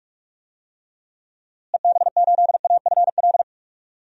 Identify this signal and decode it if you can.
Morse code, heard as a keyed tone, about 47 wpm, 700 Hz. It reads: EB8AFL
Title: DXCC-100 Call Signs - 6 Characters - Repeated 2x - 45wpm